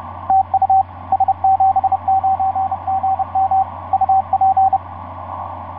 Morse code, heard as a keyed tone, about 30 wpm, 770 Hz, with radio noise. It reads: TU S79GM UP
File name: s79gm f6eyb etc 12m cw 24mar2010 1207